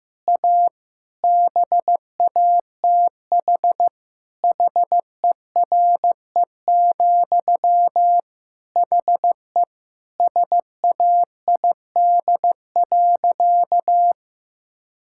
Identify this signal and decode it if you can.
Morse code, heard as a keyed tone, about 16 wpm, 700 Hz. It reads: A BATH HERE, HE SAID.
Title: Sentences from Top 1600 Words 15wpm